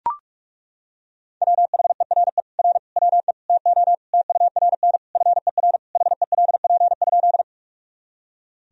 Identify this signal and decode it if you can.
Morse code, heard as a keyed tone, about 42 wpm, 700 Hz. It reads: WHERE R WE TO TURN FER HELP?
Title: Sentences from Top 200 Words and Abbreviations 45wpm